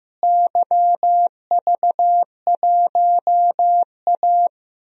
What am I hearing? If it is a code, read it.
Morse code, heard as a keyed tone, about 15 wpm, 700 Hz. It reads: YV1A
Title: DXCC-100 Call Signs - 4 Characters - Repeated 2x - 15wpm